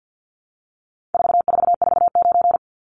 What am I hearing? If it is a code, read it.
Morse code, heard as a keyed tone, about 50 wpm, 700 Hz. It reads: 4449
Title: Numbers - 4-Digit 50wpm